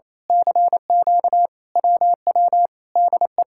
Morse code, heard as a keyed tone, about 27 wpm, 700 Hz. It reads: CQ WW DE
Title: Speed-Racing Common QSO Phrases - 28wpm